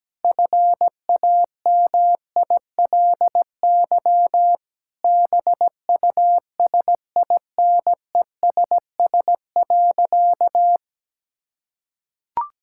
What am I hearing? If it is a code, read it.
Morse code, heard as a keyed tone, about 17 wpm, 700 Hz. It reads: FAMILY BUSINESS.
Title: Sentences from Top 1000 Words - Review Part 6 - 17wpm